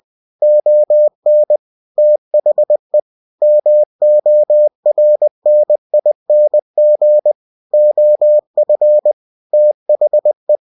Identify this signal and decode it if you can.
Morse code, heard as a keyed tone, about 20 wpm, 600 Hz. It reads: ON THE MORNING OF THE
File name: cw-books / winnie-the-pooh / winnie-20wpm-page-0096